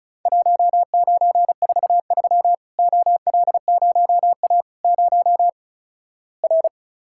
Morse code, heard as a keyed tone, about 36 wpm, 700 Hz. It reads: 1943 OL0A 0 R
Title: Worked All Europe DX Contest - QTC Exchange - 35wpm